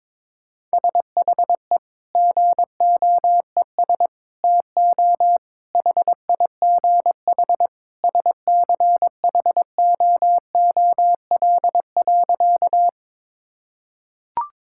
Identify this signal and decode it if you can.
Morse code, heard as a keyed tone, about 21 wpm, 700 Hz. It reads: SHE GOES TO HIGH SCHOOL.
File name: practice / 22wpm / Sentences with 2-Word Phrases - 22wpm